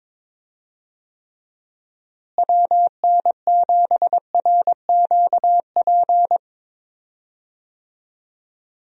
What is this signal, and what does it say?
Morse code, heard as a keyed tone, about 22 wpm, 700 Hz. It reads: WN7RQP
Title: USA Call Signs - 6 Characters - Repeated 2x - 22wpm